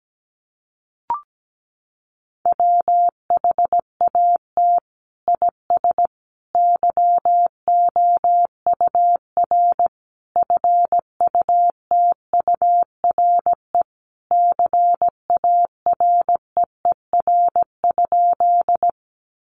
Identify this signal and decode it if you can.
Morse code, heard as a keyed tone, about 17 wpm, 700 Hz. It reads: WHAT IS YOUR FUTURE CAREER?
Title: Sentences from Top 1000 Words 17wpm